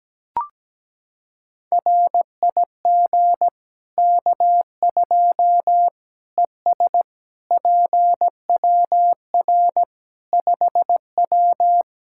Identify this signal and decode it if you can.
Morse code, heard as a keyed tone, about 17 wpm, 700 Hz. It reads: RIG K2 ES PWR 5W
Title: Practice QSO - Exchange 2 - Info - 17wpm